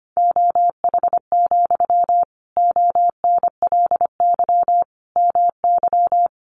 Morse code, heard as a keyed tone, about 25 wpm, 700 Hz. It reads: OH, ONLY MY
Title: Sentences from Top 1000 Words 25wpm